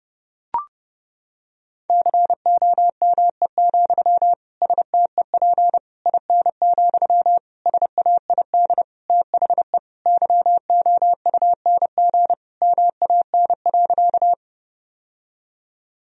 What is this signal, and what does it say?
Morse code, heard as a keyed tone, about 30 wpm, 700 Hz. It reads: COME, STEP IN, SAID THE YOUNG MAN.
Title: Sentences from Top 1000 Words - Review Part 2 - 30wpm